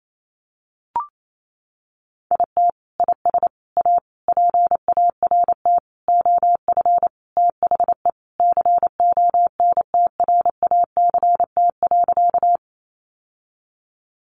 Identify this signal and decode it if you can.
Morse code, heard as a keyed tone, about 28 wpm, 700 Hz. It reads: IT IS A PART OF THE CONTRACT.
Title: Sentences from Top 1400 Words 28wpm